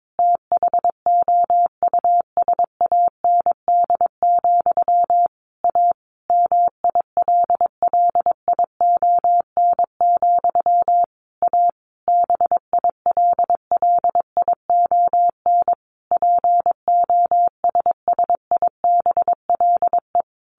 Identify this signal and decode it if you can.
Morse code, heard as a keyed tone, about 22 wpm, 700 Hz. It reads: THOUSAND, A MILLION, A BILLION POSSIBLE
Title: A Sound Of Thunder 22wpm